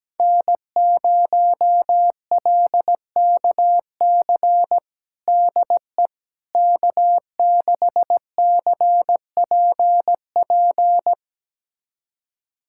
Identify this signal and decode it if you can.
Morse code, heard as a keyed tone, about 17 wpm, 700 Hz. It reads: N0LKC DE K6CPP
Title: USA Callsign DE Callsign - 5 Characters - 17wpm